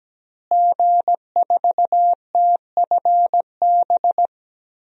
Morse code, heard as a keyed tone, about 17 wpm, 700 Hz. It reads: G4TFB
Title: DXCC-100 Call Signs - 5 Characters - Repeated 3x - 17wpm